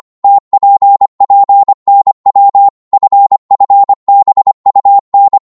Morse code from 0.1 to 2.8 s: TPPNW